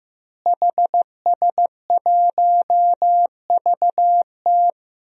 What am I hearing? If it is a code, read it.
Morse code, heard as a keyed tone, about 15 wpm, 700 Hz. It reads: HS1VT
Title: DXCC-100 Call Signs - 5 Characters - Repeated 3x - 15wpm